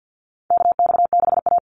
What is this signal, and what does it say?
Morse code, heard as a keyed tone, about 50 wpm, 700 Hz. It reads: KX6A
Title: NA SOTA Chaser Callsigns 50wpm